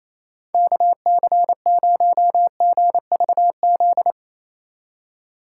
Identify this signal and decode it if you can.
Morse code, heard as a keyed tone, about 27 wpm, 700 Hz. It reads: KC0GVZ
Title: USA Call Signs - 3x Character Spacing - 28wpm